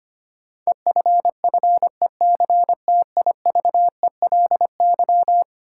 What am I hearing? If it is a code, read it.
Morse code, heard as a keyed tone, about 25 wpm, 700 Hz. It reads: EFFECTIVELY
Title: Sentences from Top 1400 Words - New Word Drill - 25wpm